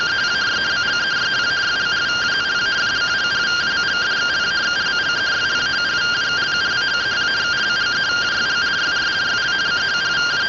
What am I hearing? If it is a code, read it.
Two-tone teleprinter signal, about 170 Hz shift.